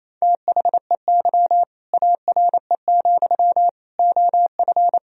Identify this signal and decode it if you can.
Morse code, heard as a keyed tone, about 27 wpm, 700 Hz. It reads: THEY ARE, OF
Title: Sentences from Top 1000 Words - Review Part 4 - 28wpm